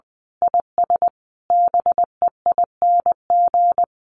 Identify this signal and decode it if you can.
Morse code, heard as a keyed tone, about 20 wpm, 700 Hz. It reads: IS BEING
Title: Sentences from Top 1500 Words 20wpm